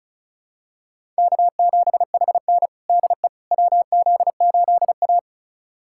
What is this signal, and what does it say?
Morse code, heard as a keyed tone, about 36 wpm, 700 Hz. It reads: K7HN DE WZ8A
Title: USA Callsign DE Callsign - 4 Characters - 35wpm